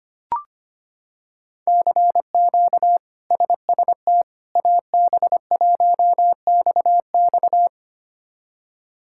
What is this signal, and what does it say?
Morse code, heard as a keyed tone, about 25 wpm, 700 Hz. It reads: CQ SST AB1XX
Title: K1USN SST QSO Practice - 25wpm